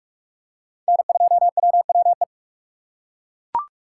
Morse code, heard as a keyed tone, about 42 wpm, 700 Hz. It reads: N2WWE
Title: USA Call Signs - 2x Character Spacing - 45wpm